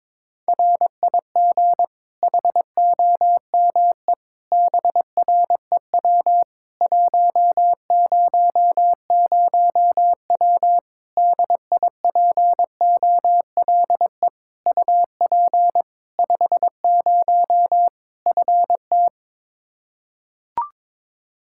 Morse code, heard as a keyed tone, about 22 wpm, 700 Hz. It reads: RIG HOME BREW 100W DIPOLE UP 50 FT
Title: Practice QSO - Exchange 2 - Info - 22wpm